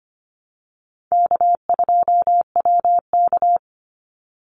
Morse code, heard as a keyed tone, about 25 wpm, 700 Hz. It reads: K2WK
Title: Easy Call Signs Repeated 2x - 25wpm